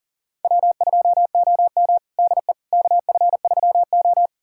Morse code, heard as a keyed tone, about 37 wpm, 700 Hz. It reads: W2OM DE KF3O
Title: USA Callsign DE Callsign - 4 Characters - 40wpm